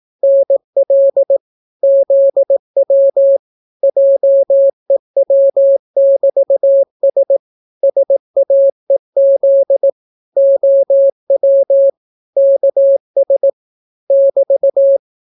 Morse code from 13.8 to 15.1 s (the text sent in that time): <BT>